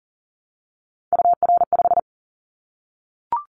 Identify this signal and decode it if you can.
Morse code, heard as a keyed tone, about 38 wpm, 700 Hz. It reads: UR5
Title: Callsign Prefixes for Top 25 DXCC - Repeated 3x - 40wpm